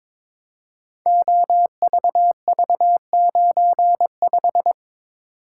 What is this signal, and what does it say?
Morse code, heard as a keyed tone, about 22 wpm, 700 Hz. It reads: OVV95
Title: Code Groups - 5 Letter:Numbers - 22wpm